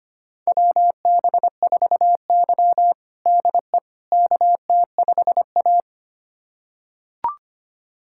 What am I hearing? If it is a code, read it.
Morse code, heard as a keyed tone, about 25 wpm, 700 Hz. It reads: WB4Y DE KT5A
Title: USA Callsign DE Callsign - 4 Characters - 25wpm